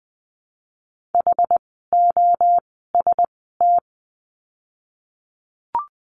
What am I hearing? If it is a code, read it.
Morse code, heard as a keyed tone, about 20 wpm, 700 Hz. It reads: H O S T
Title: 061 Lesson 11f - Letter h - Words 20wpm